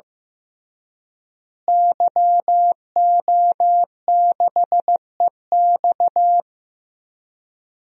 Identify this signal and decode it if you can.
Morse code, heard as a keyed tone, about 15 wpm, 700 Hz. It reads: YO6EX
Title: DXCC-100 Call Signs - 5 Characters - Repeated 3x - 15wpm